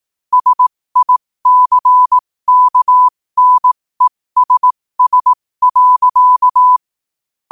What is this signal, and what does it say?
Morse code, heard as a keyed tone, about 18 wpm, spaced as for 15 wpm, 1000 Hz. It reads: SICKNESS.
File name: qotdfiles / 15wpm